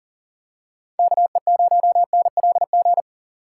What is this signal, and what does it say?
Morse code, heard as a keyed tone, about 40 wpm, 700 Hz. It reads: KE0NLG
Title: USA Call Signs - 6 Characters - 3x Character Spacing - 40wpm